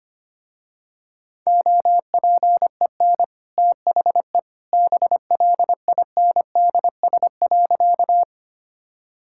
Morse code, heard as a keyed tone, about 25 wpm, 700 Hz. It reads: OPEN THE BLINDS.
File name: Easy Sentences with Hints - 25wpm